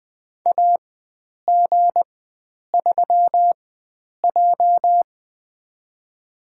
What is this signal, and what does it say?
Morse code, heard as a keyed tone, about 20 wpm, spaced as for 9 wpm, 700 Hz. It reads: AG3J
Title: USA Call Signs - 4 Characters - 4x Character Spacing - 20wpm